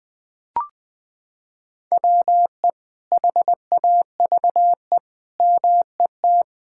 Morse code, heard as a keyed tone, about 21 wpm, 700 Hz. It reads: WE HAVE MET
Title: Sets of 9 Words - 20wpm